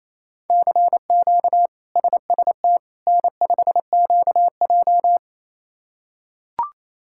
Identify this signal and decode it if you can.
Morse code, heard as a keyed tone, about 27 wpm, 700 Hz. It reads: CQ SST N5QJ
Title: K1USN SST QSO Practice - 28wpm